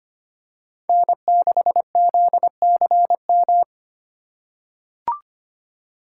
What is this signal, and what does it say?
Morse code, heard as a keyed tone, about 25 wpm, 700 Hz. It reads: N6ZCM
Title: USA Call Signs - 5 Characters - 8x Character Spacing - 25wpm